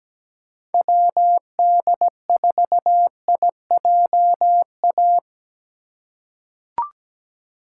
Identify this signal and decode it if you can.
Morse code, heard as a keyed tone, about 17 wpm, 700 Hz. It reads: WD4IJA
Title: USA Call Signs - 6 Characters - 4x Character Spacing - 17wpm